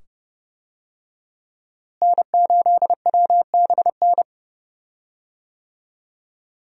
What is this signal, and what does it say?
Morse code, heard as a keyed tone, about 30 wpm, 700 Hz. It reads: N8WBN
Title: USA Call Signs - 5 Characters - Repeated 2x - 30wpm